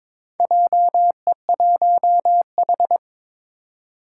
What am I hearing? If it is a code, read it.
Morse code, heard as a keyed tone, about 22 wpm, 700 Hz. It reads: JE1H
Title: Four Challenging Letter-Number Combinations 22wpm